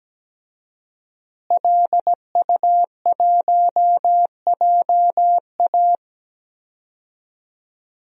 Morse code, heard as a keyed tone, about 17 wpm, 700 Hz. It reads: LU1JA
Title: DXCC-100 Call Signs - 5 Characters - Repeated 3x - 17wpm